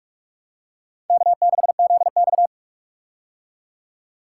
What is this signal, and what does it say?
Morse code, heard as a keyed tone, about 41 wpm, 700 Hz. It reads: K6ZX
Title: K1USN SST QSO Practice - 45wpm